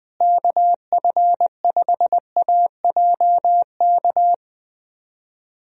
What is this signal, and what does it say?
Morse code, heard as a keyed tone, about 20 wpm, 700 Hz. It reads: KF5AJK
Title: USA Call Signs - 6 Characters - Repeated 2x - 20wpm